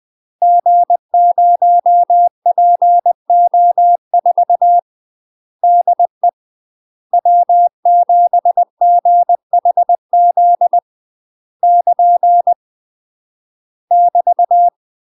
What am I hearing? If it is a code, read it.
Morse code, heard as a keyed tone, about 20 wpm, 700 Hz. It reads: G0PO4 DE W7GHZ <KN> <BT>
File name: UK MASTER QSOs TRAINING / 20WPM / UK MASTER QSOs 0044